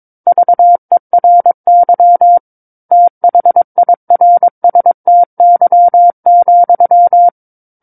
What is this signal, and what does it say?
Morse code, heard as a keyed tone, about 22 wpm, 700 Hz. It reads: VERY THIRSTY,
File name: Book One / 13-Chapter 13 - 22 WPM